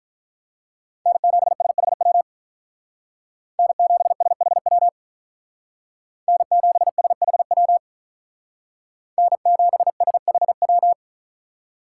Morse code, heard as a keyed tone, about 43 wpm, 700 Hz. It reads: N7SHW N7SHW N7SHW N7SHW
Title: Speed Racing - USA 1x3 Call Signs 35wpm